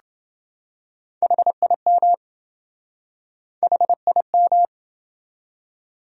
Morse code, heard as a keyed tone, about 27 wpm, 700 Hz. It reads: HIM HIM